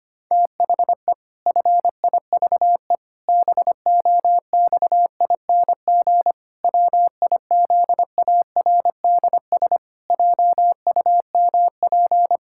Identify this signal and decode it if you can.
Morse code, heard as a keyed tone, about 25 wpm, 700 Hz. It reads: THE FIVE BOXING WIZARDS JUMP
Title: Pangram Challenge v2 - 25wpm